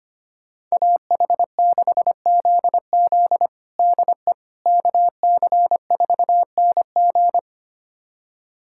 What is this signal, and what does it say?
Morse code, heard as a keyed tone, about 25 wpm, 700 Hz. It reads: AH6ZZ DE KC4NG
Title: USA Callsign DE Callsign - 5 Characters - 25wpm